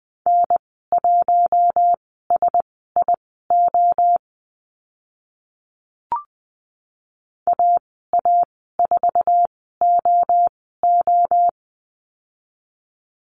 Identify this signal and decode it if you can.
Morse code, heard as a keyed tone, about 20 wpm, spaced as for 14 wpm, 700 Hz. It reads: N1SIO AA4OO
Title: 055 Lesson 10g - Letter r - Call Signs 20wpm